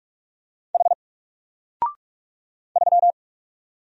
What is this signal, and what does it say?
Morse code, heard as a keyed tone, about 44 wpm, 700 Hz. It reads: H 3
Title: Single Character (Letter:Number:Punctuation:Prosign) 45wpm